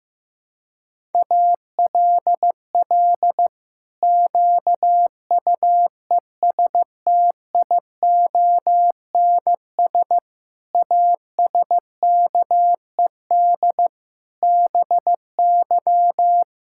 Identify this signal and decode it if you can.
Morse code, heard as a keyed tone, about 15 wpm, 700 Hz. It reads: ALL QUESTIONS ASKED BY
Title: Pangram Challenge v2 - 15wpm